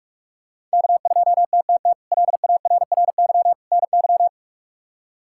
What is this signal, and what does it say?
Morse code, heard as a keyed tone, about 46 wpm, 700 Hz. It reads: K2TTT LARRY NY
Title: K1USN SST QSO Practice - 45wpm